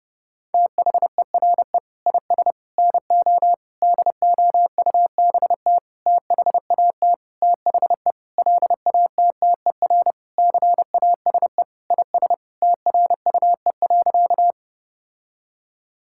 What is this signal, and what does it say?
Morse code, heard as a keyed tone, about 30 wpm, 700 Hz. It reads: THERE IS NO DOUBT THAT THE LATTER CASE IS TRUE.
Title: Sentences from Top 1600 Words 30wpm